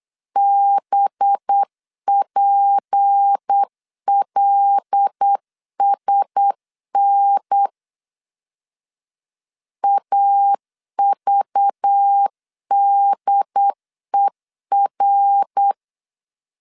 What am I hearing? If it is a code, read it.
Morse code, heard as a keyed tone, about 8 wpm, 800 Hz. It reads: BPLSN AVDER
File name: LEZIONI / 8 YJ45 / 8 Y 8wpm 3245